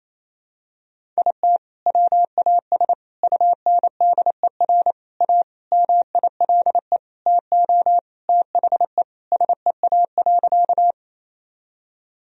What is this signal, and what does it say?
Morse code, heard as a keyed tone, about 28 wpm, 700 Hz. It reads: IT WAS UNDER A MILE TO THE SEA.
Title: Sentences from Top 1000 Words - Review Part 1 - 28wpm